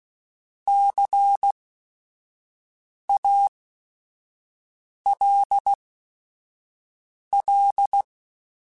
Morse code, heard as a keyed tone, about 16 wpm, spaced as for 5 wpm, 800 Hz. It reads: CALL
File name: QSOs / QSOs-mp3s / kc7sac@5